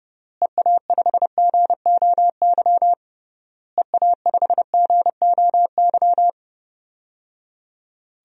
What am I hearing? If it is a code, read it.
Morse code, heard as a keyed tone, about 30 wpm, 700 Hz. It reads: EA5GOY EA5GOY
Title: DXCC-100 Call Signs - 6 Characters - Repeated 3x - 30wpm